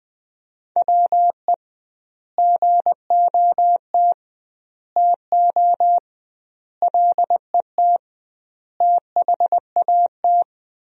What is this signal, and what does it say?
Morse code, heard as a keyed tone, about 21 wpm, 700 Hz. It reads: WE GOT TO LET THAT
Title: 2x Word-Spacing - Sentences from Top 300 Words - 20wpm